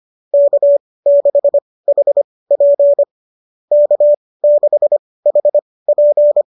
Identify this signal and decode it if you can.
Morse code, heard as a keyed tone, about 25 wpm, spaced as for 18 wpm, 590 Hz. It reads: K6HP K6HP